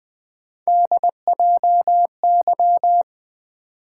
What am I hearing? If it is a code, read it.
Morse code, heard as a keyed tone, about 20 wpm, 700 Hz. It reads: DJY
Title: Three Letter-Number Combinations - 20wpm